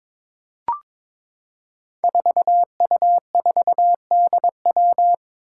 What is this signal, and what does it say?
Morse code, heard as a keyed tone, about 22 wpm, 700 Hz. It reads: 4U4DW
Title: Five Challenging Letter-Number Combinations 22wpm